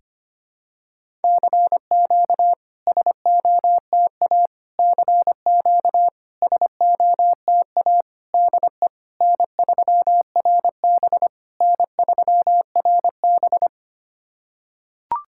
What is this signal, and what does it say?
Morse code, heard as a keyed tone, about 25 wpm, 700 Hz. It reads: CQ SOTA CQ SOTA DE N3RB N3RB
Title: SOTA QSO Practice 25wpm